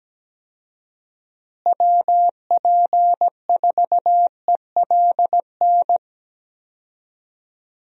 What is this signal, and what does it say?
Morse code, heard as a keyed tone, about 17 wpm, 700 Hz. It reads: WP4ELN
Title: USA Call Signs - 6 Characters - Repeated 3x - 17wpm